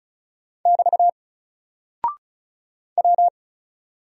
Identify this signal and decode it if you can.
Morse code, heard as a keyed tone, about 36 wpm, 700 Hz. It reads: <BT> W
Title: Single Character (Letter:Number:Punctuation:Prosign) Alt 35wpm